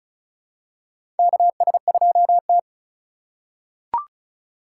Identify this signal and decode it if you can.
Morse code, heard as a keyed tone, about 36 wpm, 700 Hz. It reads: KS2T
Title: USA Call Signs - 8x Character Spacing - 35wpm